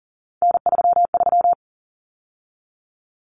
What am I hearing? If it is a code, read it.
Morse code, heard as a keyed tone, about 40 wpm, 700 Hz. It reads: N33
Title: Common Cut Numbers - 3-Digit 40wpm